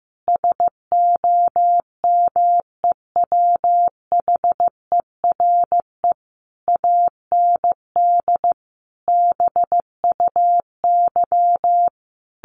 Morse code, heard as a keyed tone, about 15 wpm, 700 Hz. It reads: SOMEWHERE AND BUY
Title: Sentences from Top 1400 Words 15wpm